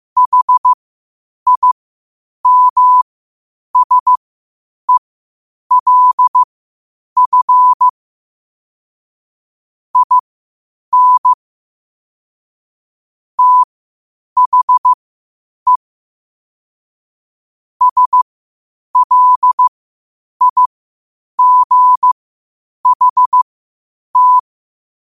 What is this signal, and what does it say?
Morse code, heard as a keyed tone, about 15 wpm, spaced as for 8 wpm, 1000 Hz. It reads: HIMSELF IN THE SLIGHT
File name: qotdfiles / 8wpm